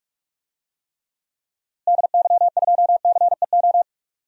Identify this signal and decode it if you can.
Morse code, heard as a keyed tone, about 46 wpm, 700 Hz. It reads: DY2CEO
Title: DXCC-100 Call Signs - 6 Characters - Repeated 3x - 45wpm